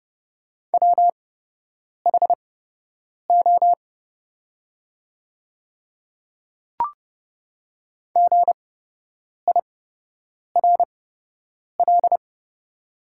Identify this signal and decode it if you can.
Morse code, heard as a keyed tone, about 30 wpm, spaced as for 8 wpm, 700 Hz. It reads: WHO GIRL
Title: 8x Character Spacing - Top 500 Words - 30wpm